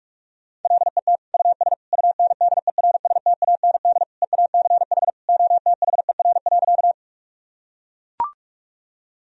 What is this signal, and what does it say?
Morse code, heard as a keyed tone, about 41 wpm, 700 Hz. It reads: LET US UNDERSTAND EACH OTHER.
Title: Sentences from Top 1000 Words - Review Part 1 - 45wpm